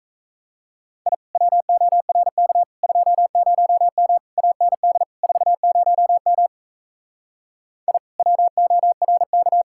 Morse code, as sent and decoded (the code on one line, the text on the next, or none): .. / .-- --- .-. -.- / ..--- ----- -- / .- -. -.. / ....- ----- -- / .. / .-- --- .-. -.-
I WORK 20M AND 40M I WORK